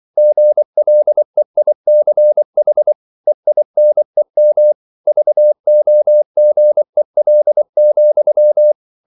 Morse code, heard as a keyed tone, about 24 wpm, 600 Hz. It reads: GLEICH EINEM VOGEL,